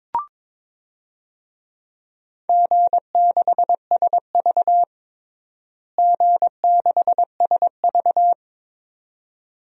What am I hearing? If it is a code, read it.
Morse code, heard as a keyed tone, about 22 wpm, 700 Hz. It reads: G6SV G6SV
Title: DXCC-100 Call Signs - 4 Characters - Repeated 2x - 22wpm